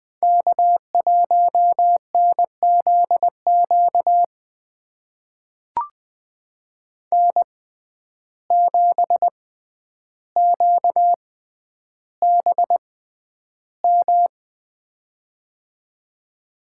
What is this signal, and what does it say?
Morse code, heard as a keyed tone, about 20 wpm, spaced as for 10 wpm, 700 Hz. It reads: K1NZQ N7QBM
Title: USA Call Signs - 5 Characters - 6x Character Spacing - 20wpm